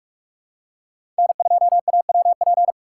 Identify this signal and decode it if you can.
Morse code, heard as a keyed tone, about 44 wpm, 700 Hz. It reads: N2AWP
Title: USA Call Signs - 5 Characters - 6x Character Spacing - 45wpm